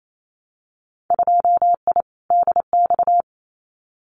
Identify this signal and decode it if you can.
Morse code, heard as a keyed tone, about 28 wpm, 700 Hz. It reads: 2I DX
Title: Winter Field Day Exchanges 28wpm